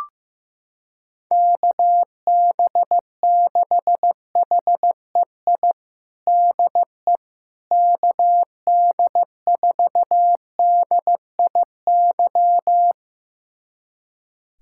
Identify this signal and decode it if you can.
Morse code, heard as a keyed tone, about 15 wpm, 700 Hz. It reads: KB6HEI DE KD4DIY
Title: USA Callsign DE Callsign - 6 Characters - 15wpm